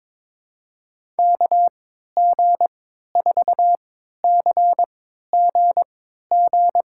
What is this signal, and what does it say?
Morse code, heard as a keyed tone, about 22 wpm, spaced as for 13 wpm, 700 Hz. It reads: KG4CGG
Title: USA Call Signs - 3x Character Spacing - 22wpm